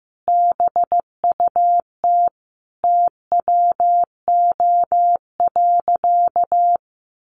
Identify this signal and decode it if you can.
Morse code, heard as a keyed tone, about 15 wpm, 700 Hz. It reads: BUT TWO.
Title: Sentences from Top 1400 Words 15wpm